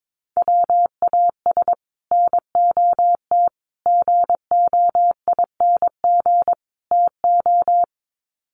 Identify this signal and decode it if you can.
Morse code, heard as a keyed tone, about 22 wpm, 700 Hz. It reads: WAS NOT GOING TO
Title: Sets of 4 Words 22wpm